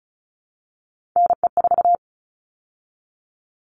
Morse code, heard as a keyed tone, about 34 wpm, 700 Hz. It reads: NE4
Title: All Cut Numbers - 3-Digit 35wpm